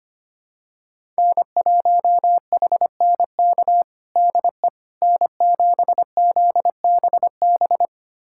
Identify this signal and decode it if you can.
Morse code, heard as a keyed tone, about 25 wpm, 700 Hz. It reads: N1HNK DE N7ZBB
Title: USA Callsign DE Callsign - 5 Characters - 25wpm